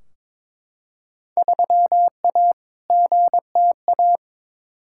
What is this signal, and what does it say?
Morse code, heard as a keyed tone, about 22 wpm, 700 Hz. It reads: 3A GTA
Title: Field Day Exchanges - 22wpm